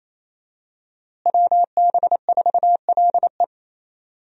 Morse code, heard as a keyed tone, about 27 wpm, 700 Hz. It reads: WB4LE
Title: USA Call Signs - 5 Characters - Repeated 2x - 28wpm